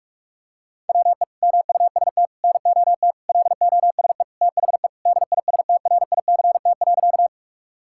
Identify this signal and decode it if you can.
Morse code, heard as a keyed tone, about 46 wpm, 700 Hz. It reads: WE MUST NOT LOSE THE DISTRICT.
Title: Sentences from Top 1600 Words 45wpm